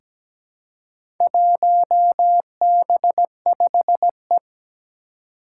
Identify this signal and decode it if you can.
Morse code, heard as a keyed tone, about 17 wpm, 700 Hz. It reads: 1B5E
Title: Four Challenging Letter-Number Combinations 17wpm